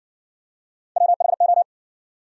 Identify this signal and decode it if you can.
Morse code, heard as a keyed tone, about 60 wpm, 700 Hz. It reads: WHO